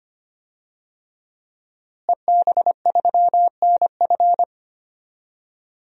Morse code, heard as a keyed tone, about 25 wpm, 700 Hz. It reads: EB3NF
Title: DXCC-100 Call Signs - 5 Characters - Repeated 3x - 25wpm